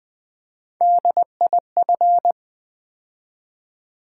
Morse code, heard as a keyed tone, about 20 wpm, 700 Hz. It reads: DIF
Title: Build A Word - 3 Letters At A Time - 9-Letter Words - 20wpm